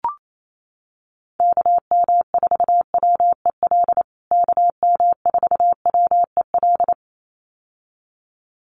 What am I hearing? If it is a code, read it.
Morse code, heard as a keyed tone, about 28 wpm, 700 Hz. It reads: KM4WEL KM4WEL
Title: USA Call Signs Repeated 2x 28wpm